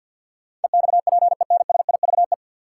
Morse code, heard as a keyed tone, about 45 wpm, 700 Hz. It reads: EXPENSIVE
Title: Sentences from Top 1500 Words - New Word Drill Alt - 50wpm